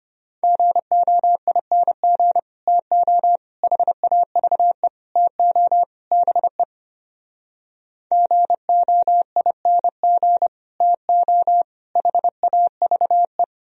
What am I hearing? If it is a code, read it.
Morse code, heard as a keyed tone, about 33 wpm, 700 Hz. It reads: GOING TO HAVE TO BE GOING TO HAVE